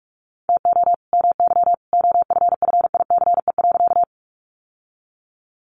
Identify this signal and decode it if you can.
Morse code, heard as a keyed tone, about 44 wpm, 700 Hz. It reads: TO MY OFFICE.
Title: Sentences from Top 1000 Words 45wpm